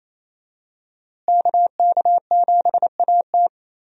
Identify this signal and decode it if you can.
Morse code, heard as a keyed tone, about 27 wpm, 700 Hz. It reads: KK7AT
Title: USA Call Signs - 5 Characters - Repeated 3x - 28wpm